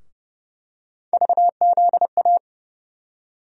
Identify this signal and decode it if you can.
Morse code, heard as a keyed tone, about 30 wpm, 700 Hz. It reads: VZA
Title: USA Call Sign Postfixes - 3 Letters 30wpm